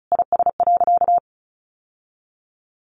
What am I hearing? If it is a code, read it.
Morse code, heard as a keyed tone, about 34 wpm, 700 Hz. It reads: IS.
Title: Sentences from Top 1100 Words 35wpm